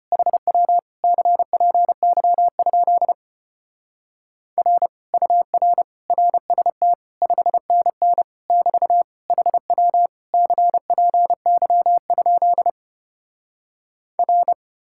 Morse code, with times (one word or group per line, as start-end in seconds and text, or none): 0.1-0.8 HW
1.0-3.1 CPY?
4.6-4.9 R
5.1-5.8 UR
6.1-6.9 RST
7.2-8.2 5NN
8.5-9.0 <BT>
9.3-10.1 HW
10.3-12.7 CPY?
14.2-14.5 R